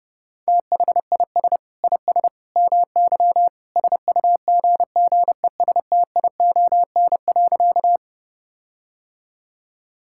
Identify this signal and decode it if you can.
Morse code, heard as a keyed tone, about 30 wpm, 700 Hz. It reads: THIS IS MY SUGGESTION.
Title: Sentences from Top 1600 Words 30wpm